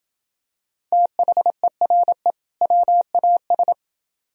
Morse code, heard as a keyed tone, about 29 wpm, 700 Hz. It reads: THERE WAS